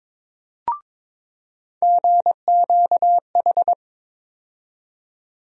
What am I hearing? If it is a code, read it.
Morse code, heard as a keyed tone, about 22 wpm, 700 Hz. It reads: GQH
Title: USA Call Sign Postfixes - 3 Letters 22wpm